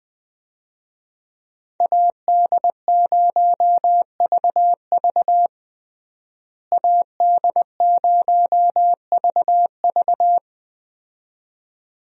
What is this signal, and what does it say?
Morse code, heard as a keyed tone, about 20 wpm, 700 Hz. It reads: AD0VV AD0VV
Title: USA Call Signs - 5 Characters - Repeated 2x - 20wpm